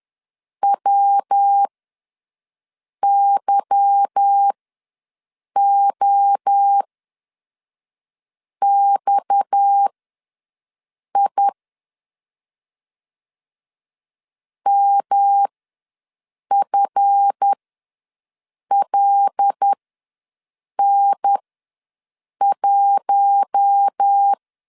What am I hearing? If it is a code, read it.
Morse code, heard as a keyed tone, about 10 wpm, spaced as for 5 wpm, 800 Hz. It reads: WYOXI MFLN1
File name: LEZIONI / 8 YJ45 / 8 YJ4 ws 10wpm 4020